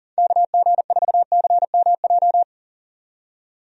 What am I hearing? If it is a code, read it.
Morse code, heard as a keyed tone, about 39 wpm, 700 Hz. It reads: KG4CMJ
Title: USA Call Signs - 6 Characters - 40wpm